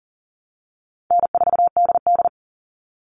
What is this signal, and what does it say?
Morse code, heard as a keyed tone, about 37 wpm, 700 Hz. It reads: N4DD
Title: Easy Call Signs Repeated 2x - 40wpm